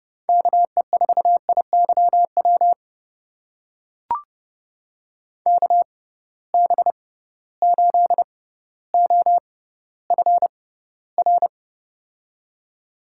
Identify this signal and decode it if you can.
Morse code, heard as a keyed tone, about 30 wpm, spaced as for 15 wpm, 700 Hz. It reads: KE4IYW KB8OFR
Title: USA Call Signs - 6 Characters - 6x Character Spacing - 30wpm